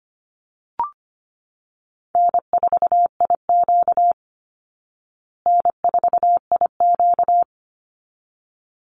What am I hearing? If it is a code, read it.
Morse code, heard as a keyed tone, about 25 wpm, 700 Hz. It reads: N4IQ N4IQ
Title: Call Signs Repeated 3x - 25wpm